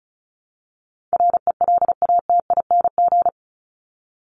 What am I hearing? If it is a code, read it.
Morse code, heard as a keyed tone, about 34 wpm, 700 Hz. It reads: RELATING
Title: ING Suffix - Top 100 Words with Base Words 35wpm